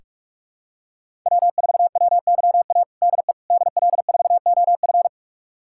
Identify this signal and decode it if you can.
Morse code, heard as a keyed tone, about 44 wpm, 700 Hz. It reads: W4WYA DE DL4OF
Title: USA Callsign DE Callsign - 5 Characters - 45wpm